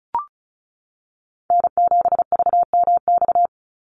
Morse code, heard as a keyed tone, about 34 wpm, 700 Hz. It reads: N7VMX
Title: USA Call Signs 35wpm